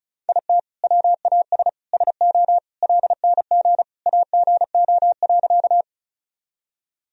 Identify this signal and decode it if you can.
Morse code, heard as a keyed tone, about 36 wpm, 700 Hz. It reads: IT WAS SO LNG AGO.
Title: Sentences from Top 500 Words and Abbreviations 35wpm